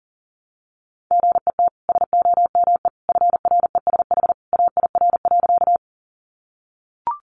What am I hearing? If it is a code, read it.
Morse code, heard as a keyed tone, about 38 wpm, 700 Hz. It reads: GET SOME FRESH AIR.
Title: Sentences from Top 1200 Words 40wpm